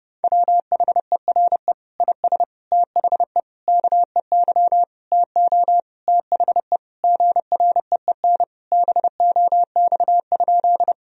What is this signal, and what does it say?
Morse code, heard as a keyed tone, about 30 wpm, 700 Hz. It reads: WHERE IS THE KEY TO THE GREEN BOX?
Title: Sentences from Top 1000 Words - Review Part 4 - 30wpm